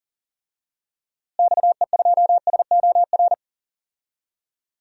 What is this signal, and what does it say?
Morse code, heard as a keyed tone, about 40 wpm, 700 Hz. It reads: XE2SOR
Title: DXCC-100 Call Signs - 6 Characters - Repeated 2x - 40wpm